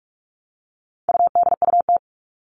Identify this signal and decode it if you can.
Morse code, heard as a keyed tone, about 44 wpm, 700 Hz. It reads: UDUT
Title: All Cut Numbers - 4-Digit 45wpm